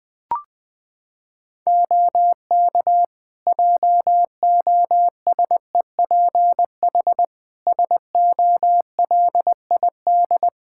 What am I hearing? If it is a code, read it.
Morse code, heard as a keyed tone, about 20 wpm, 700 Hz. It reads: OK JOSEPH SOLID
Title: Practice QSO - Exchange 3 - More Info - 20wpm